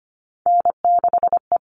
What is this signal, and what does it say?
Morse code, heard as a keyed tone, about 25 wpm, 700 Hz. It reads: N6E
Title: All Cut Numbers - 3-Digit 25wpm